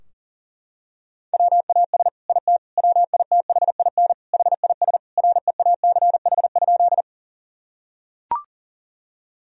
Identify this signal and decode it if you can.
Morse code, heard as a keyed tone, about 40 wpm, 700 Hz. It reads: WAS IT WITHIN HIS REACH?
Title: Sentences from Top 1000 Words - Review Part 4 - 40wpm